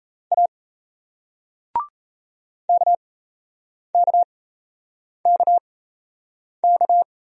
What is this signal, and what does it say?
Morse code, heard as a keyed tone, about 38 wpm, spaced as for 8 wpm, 700 Hz. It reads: A KKKK